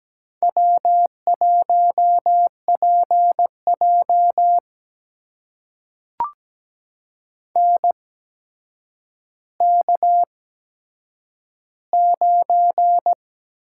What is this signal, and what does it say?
Morse code, heard as a keyed tone, about 17 wpm, spaced as for 8 wpm, 700 Hz. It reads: W1PJ NK9
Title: USA Call Signs - 4 Characters - 8x Character Spacing - 17wpm